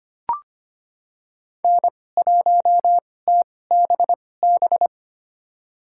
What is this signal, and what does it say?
Morse code, heard as a keyed tone, about 25 wpm, spaced as for 18 wpm, 700 Hz. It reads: N1TBB
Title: USA Call Signs - 5 Characters - 2x Character Spacing - 25wpm